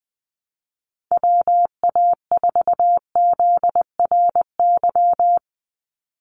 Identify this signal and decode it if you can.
Morse code, heard as a keyed tone, about 20 wpm, 700 Hz. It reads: WA4ZRY
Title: USA Call Signs 20wpm